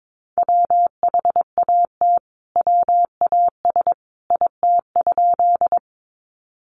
Morse code, heard as a keyed tone, about 22 wpm, 700 Hz. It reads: WHAT WAS IT?
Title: Sentences from Top 100 Words 22wpm